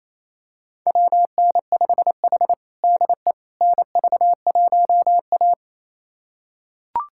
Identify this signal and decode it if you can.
Morse code, heard as a keyed tone, about 27 wpm, 700 Hz. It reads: WN5H DE NV1A
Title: USA Callsign DE Callsign - 4 Characters - 28wpm